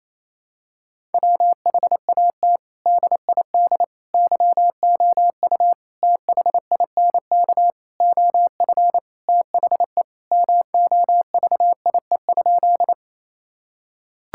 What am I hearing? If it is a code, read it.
Morse code, heard as a keyed tone, about 28 wpm, 700 Hz. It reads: WHAT DID YOU THINK OF THE MOVIE?
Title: Sentences from Top 1000 Words - Review Part 5 - 28wpm